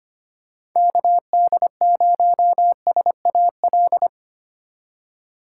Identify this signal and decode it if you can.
Morse code, heard as a keyed tone, about 25 wpm, 700 Hz. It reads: KD0SAL
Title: USA Call Signs - 6 Characters - 3x Character Spacing - 25wpm